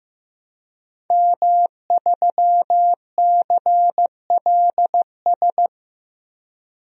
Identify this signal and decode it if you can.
Morse code, heard as a keyed tone, about 15 wpm, 700 Hz. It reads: M3CLS
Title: DXCC-100 Call Signs - 5 Characters - 15wpm